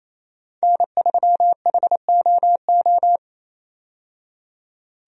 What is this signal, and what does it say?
Morse code, heard as a keyed tone, about 28 wpm, 700 Hz. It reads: N3HOO
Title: USA Call Signs - 5 Characters - 28wpm